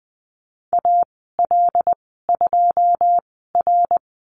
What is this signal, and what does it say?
Morse code, heard as a keyed tone, about 20 wpm, spaced as for 14 wpm, 700 Hz. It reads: AL2R
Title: 082 Lesson 14g - Number 2 - Call Signs 20wpm